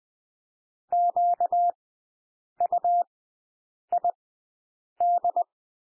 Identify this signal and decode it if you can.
Morse code, heard as a keyed tone, about 20 wpm, spaced as for 8 wpm, 700 Hz. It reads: QUID
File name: cwmp3 / W1 SOUND WORDS 20 wpm